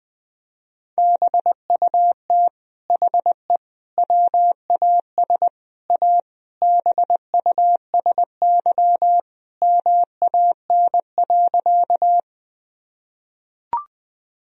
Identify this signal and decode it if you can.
Morse code, heard as a keyed tone, about 20 wpm, 700 Hz. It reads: BUT HE WAS A BUSY MAN.
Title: Sentences from Top 1000 Words - Review Part 3 - 20wpm